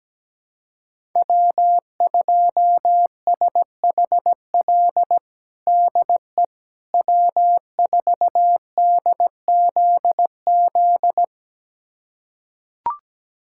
Morse code, heard as a keyed tone, about 17 wpm, 700 Hz. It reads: W2SHL DE W4DZZ
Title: USA Callsign DE Callsign - 5 Characters - 17wpm